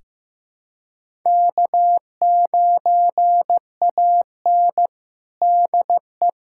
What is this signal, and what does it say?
Morse code, heard as a keyed tone, about 15 wpm, 700 Hz. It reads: K9AN DE
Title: USA Callsign DE Callsign - 4 Characters - 15wpm